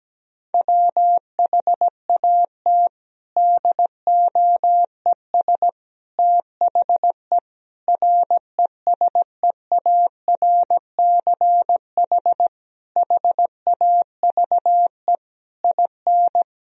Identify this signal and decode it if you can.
Morse code, heard as a keyed tone, about 17 wpm, 700 Hz. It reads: WHAT DOES THE RESEARCH HAVE IN
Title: Sentences from Top 1000 Words - Review Part 4 - 17wpm